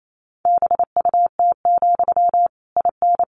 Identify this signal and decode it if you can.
Morse code, heard as a keyed tone, about 28 wpm, 700 Hz. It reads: BUT, IN
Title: Sentences from Top 1500 Words 28wpm